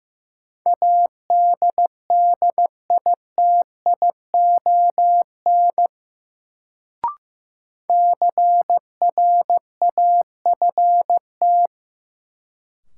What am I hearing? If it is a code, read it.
Morse code, heard as a keyed tone, about 15 wpm, 700 Hz. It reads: ADDITION CRAFT
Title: Sentences from Top 1000 Words - Review Part 6 - New Word Drill - 15wpm